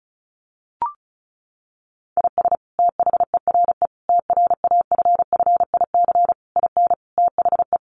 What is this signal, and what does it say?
Morse code, heard as a keyed tone, about 34 wpm, 700 Hz. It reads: IS THERE TRAFFIC IN THE
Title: Sentences from Top 1300 Words 35wpm